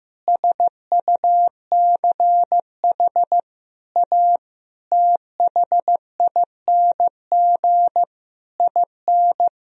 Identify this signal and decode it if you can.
Morse code, heard as a keyed tone, about 15 wpm, 700 Hz. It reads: SUCH A THING IN
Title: Sentences from Top 1000 Words - Review Part 3 - 15wpm